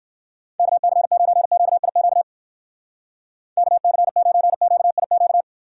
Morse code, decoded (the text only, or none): KC9QIQ KC9QIQ